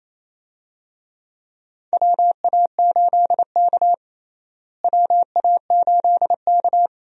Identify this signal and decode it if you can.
Morse code, heard as a keyed tone, about 28 wpm, 700 Hz. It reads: WA8K WA8K
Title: USA Call Signs - 4 Characters - Repeated 2x - 28wpm